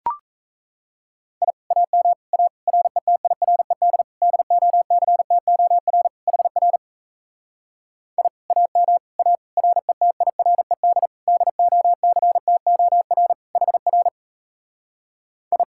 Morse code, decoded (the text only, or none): I AM A RETIRED DOCTOR HR I AM A RETIRED DOCTOR HR I